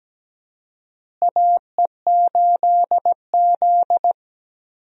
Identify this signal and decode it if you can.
Morse code, heard as a keyed tone, about 17 wpm, 700 Hz. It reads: AE8Z
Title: USA Call Signs - 4 Characters - Repeated 2x - 17wpm